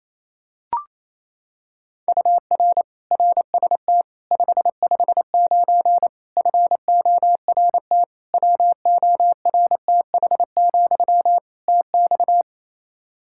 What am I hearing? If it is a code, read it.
Morse code, heard as a keyed tone, about 28 wpm, 700 Hz. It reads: UR RST 559 FORT WORTH, TX
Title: Practice QSO - Exchange 1 - Beginning - 28wpm